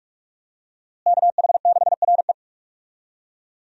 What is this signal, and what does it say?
Morse code, heard as a keyed tone, about 45 wpm, 700 Hz. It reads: KH6RE
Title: USA Call Signs - 5 Characters - Repeated 2x - 45wpm